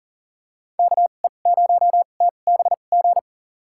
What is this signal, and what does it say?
Morse code, heard as a keyed tone, about 40 wpm, spaced as for 29 wpm, 700 Hz. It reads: KE0TBG
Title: USA Call Signs - 2x Character Spacing - 40wpm